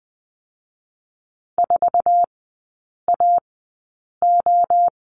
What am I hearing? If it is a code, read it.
Morse code, heard as a keyed tone, about 20 wpm, spaced as for 8 wpm, 700 Hz. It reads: 4AO
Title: 053 Lesson 10e - Letter r - Three Characters 20wpm